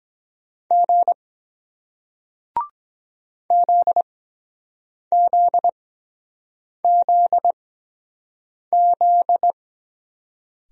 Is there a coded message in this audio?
Yes: Morse code from 0.7 to 9.5 s, heard as a keyed tone, about 23 wpm, spaced as for 7 wpm, 700 Hz. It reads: G ZZZZ